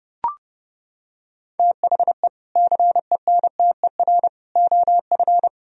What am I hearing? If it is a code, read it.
Morse code, heard as a keyed tone, about 29 wpm, 700 Hz. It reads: THE CENTER OF